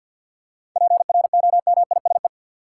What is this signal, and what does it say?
Morse code, heard as a keyed tone, about 53 wpm, 700 Hz. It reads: PROMISE
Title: Seven Letter Words - 50wpm